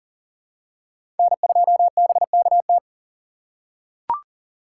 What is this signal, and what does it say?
Morse code, heard as a keyed tone, about 40 wpm, 700 Hz. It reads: N2BKT
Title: USA Call Signs - 5 Characters - 2x Character Spacing - 40wpm